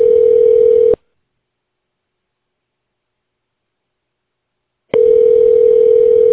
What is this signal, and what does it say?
Ringback tone (North America), 440 + 480 Hz.